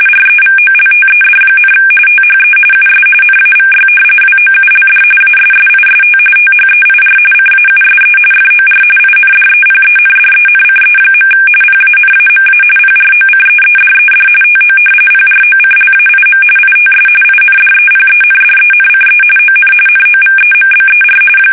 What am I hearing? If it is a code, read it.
RTTY, about 45.45 baud. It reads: REVOLTED, THE US NAVY WOULD ASSIST THEIR CAUSE FOR INDEPENDENCE. PANAMA PROCEEDED TO PROCLAIM ITS INDEPENDENCE ON NOVEMBER
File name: fsk850 / usnavy3